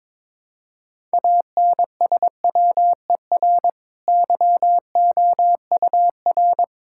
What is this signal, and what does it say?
Morse code, heard as a keyed tone, about 22 wpm, 700 Hz. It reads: ANSWER YOUR
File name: practice / 22wpm / Easy Sentences with Hints - 22wpm